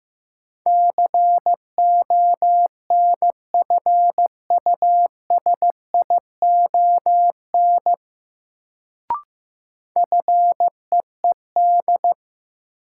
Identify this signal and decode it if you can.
Morse code, heard as a keyed tone, about 15 wpm, 700 Hz. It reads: CONFUSION FEED
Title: Sentences from Top 1600 Words - New Word Drill Alt - 15wpm